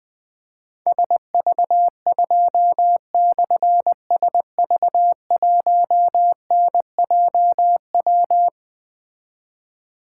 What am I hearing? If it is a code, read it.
Morse code, heard as a keyed tone, about 20 wpm, 700 Hz. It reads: SV2/SV1NJW
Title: Difficult Call Signs 20wpm